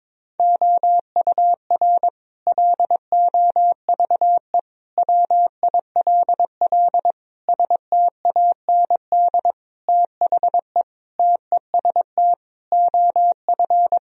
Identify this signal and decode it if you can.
Morse code, heard as a keyed tone, about 22 wpm, 700 Hz. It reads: OUR LOVE WILL STAND THE TEST OF
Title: Sentences from Top 1000 Words - Review Part 3 - 22wpm